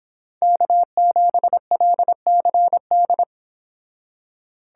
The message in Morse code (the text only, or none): K7LCD